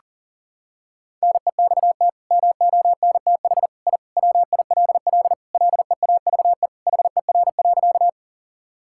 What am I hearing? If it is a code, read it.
Morse code, heard as a keyed tone, about 40 wpm, 700 Hz. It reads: NEXT MONTH I WILL LEAVE HER.
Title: Sentences from Top 1000 Words - Review Part 5 - 40wpm